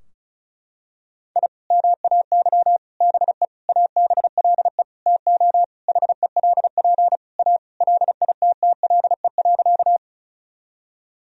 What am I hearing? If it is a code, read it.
Morse code, heard as a keyed tone, about 36 wpm, 700 Hz. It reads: I MAY BE ABLE TO HELP A LITTLE.
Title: Sentences from Top 1000 Words - Review Part 2 - 35wpm